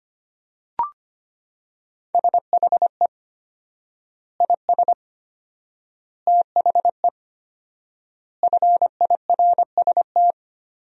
Morse code, heard as a keyed tone, about 25 wpm, 700 Hz. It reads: SHE IS THE FIRST
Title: 4x Word-Spacing - Sentences from Top 400 Words - 25wpm